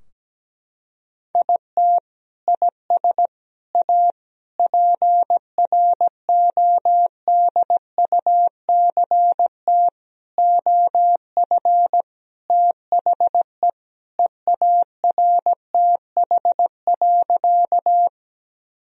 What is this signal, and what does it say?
Morse code, heard as a keyed tone, about 17 wpm, 700 Hz. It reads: IT IS A PRODUCT OF THE EARTH.
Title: Sentences from Top 1000 Words - Review Part 2 - 17wpm